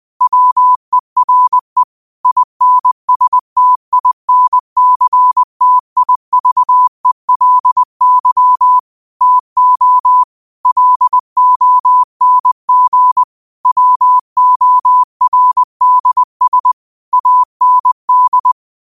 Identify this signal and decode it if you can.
Morse code, heard as a keyed tone, about 20 wpm, 1000 Hz. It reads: WERE INSTINCTIVELY TO LONG WORDS AND
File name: qotdfiles / 20wpm